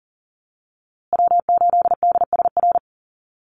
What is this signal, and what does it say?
Morse code, heard as a keyed tone, about 40 wpm, 700 Hz. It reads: W8DSR
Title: USA Call Signs Repeated 3x 40wpm